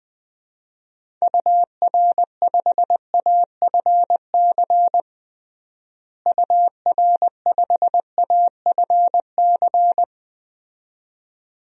Morse code, heard as a keyed tone, about 20 wpm, 700 Hz. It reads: UR5AFC UR5AFC
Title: DXCC-100 Call Signs - 6 Characters - Repeated 2x - 20wpm